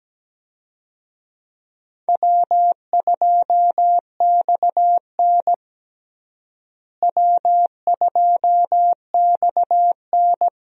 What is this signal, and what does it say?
Morse code, heard as a keyed tone, about 17 wpm, 700 Hz. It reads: W2XN W2XN
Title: USA Call Signs - 4 Characters - Repeated 2x - 17wpm